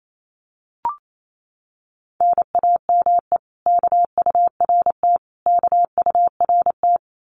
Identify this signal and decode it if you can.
Morse code, heard as a keyed tone, about 28 wpm, 700 Hz. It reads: NAME KURT KURT
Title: Common QSO Phrases Encore 28wpm